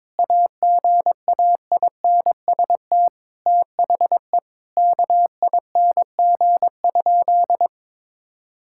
Morse code, heard as a keyed tone, about 22 wpm, 700 Hz. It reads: AGAINST THE KING?
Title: Sentences from Top 1000 Words - Review Part 6 - 22wpm